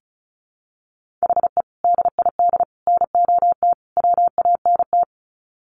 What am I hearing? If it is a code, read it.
Morse code, heard as a keyed tone, about 34 wpm, 700 Hz. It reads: HE DID NOT WANT
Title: Sets of 4 Words 35wpm